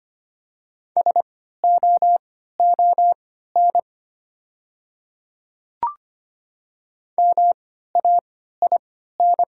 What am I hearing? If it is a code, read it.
Morse code, heard as a keyed tone, about 25 wpm, spaced as for 14 wpm, 690 Hz. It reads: SOON MAIN
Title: 3x Character Spacing - Top 500 Words - 25wpm